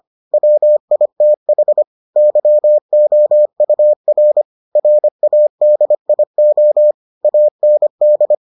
Morse code, heard as a keyed tone, about 25 wpm, 600 Hz. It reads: WITH YOUR RADIO AND